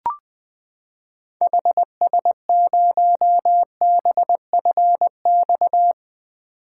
Morse code, heard as a keyed tone, about 21 wpm, 700 Hz. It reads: HS0BFX
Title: 2023 POTA Activator Call Signs - 20wpm